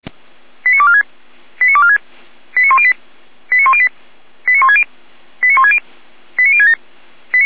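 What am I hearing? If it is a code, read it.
Five-tone selective call (ZVEI), heard as quick runs of five tones, tones of about 70 ms. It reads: 89246 89246 89108 89108 89160 89160 89067
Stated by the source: ZVEI five-tone call received on SDR.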